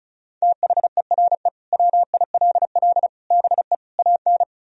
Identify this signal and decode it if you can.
Morse code, heard as a keyed tone, about 34 wpm, 700 Hz. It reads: THERE WILL BE AN
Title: Sentences from Top 1000 Words - Review Part 4 - 35wpm